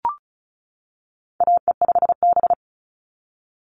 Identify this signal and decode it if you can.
Morse code, heard as a keyed tone, about 34 wpm, 700 Hz. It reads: AE5B
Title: NA SOTA Chaser Callsigns 35wpm